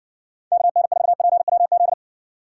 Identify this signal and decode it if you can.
Morse code, heard as a keyed tone, about 58 wpm, 690 Hz. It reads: KN4PWZ